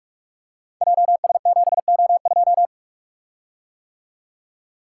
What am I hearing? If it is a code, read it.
Morse code, heard as a keyed tone, about 44 wpm, 690 Hz. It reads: JS7O2
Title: Code Groups - 5 Letter:Numbers - 45wpm